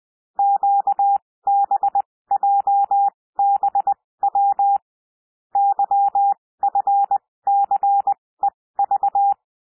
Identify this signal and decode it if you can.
Morse code, heard as a keyed tone, about 20 wpm, spaced as for 16 wpm, 800 Hz. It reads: QBJBW YFCEV
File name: cwmp3 / L1 SOUND LETTERS 20 wpm